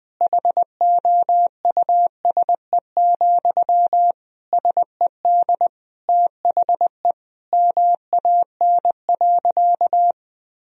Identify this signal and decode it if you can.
Morse code, heard as a keyed tone, about 20 wpm, 700 Hz. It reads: HOUSE, SED THE MAN.
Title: Sentences from Top 400 Words and Abbreviations 20wpm